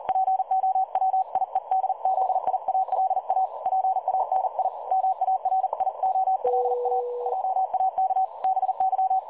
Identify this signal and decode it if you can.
Morse code, heard as a keyed tone, about 40 wpm, 760 Hz. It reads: POWIEDZIALA PANI MARIOLA JAK BY
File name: dlazaawansowanych / nagraniazpasma / websdr recording 3565.0kHz